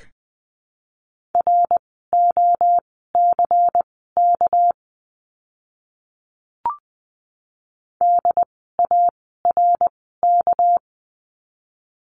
Morse code, heard as a keyed tone, about 20 wpm, spaced as for 14 wpm, 700 Hz. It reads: ROCK DARK
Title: 199 Lesson 32f - Letter k - Words 20wpm